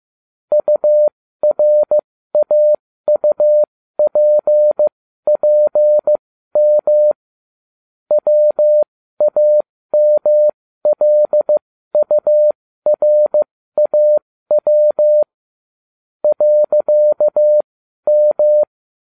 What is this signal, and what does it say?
Morse code, heard as a keyed tone, about 15 wpm, spaced as for 12 wpm, 600 Hz. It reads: URAUPPM WAMLURAW .M